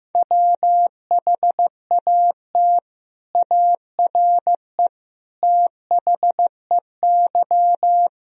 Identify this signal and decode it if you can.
Morse code, heard as a keyed tone, about 15 wpm, 700 Hz. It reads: WHAT ARE THEY
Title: Sentences from Top 1000 Words - Review Part 3 - 15wpm